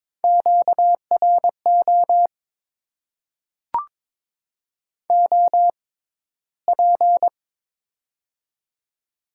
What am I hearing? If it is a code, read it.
Morse code, heard as a keyed tone, about 23 wpm, spaced as for 14 wpm, 700 Hz. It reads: QRO OP
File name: Top 90 QSO Elements - 6x Character Spacing - 22wpm